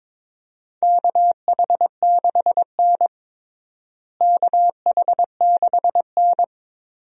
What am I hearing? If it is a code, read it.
Morse code, heard as a keyed tone, about 22 wpm, 700 Hz. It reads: KH6N KH6N
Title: DXCC-100 Call Signs - 4 Characters - Repeated 2x - 22wpm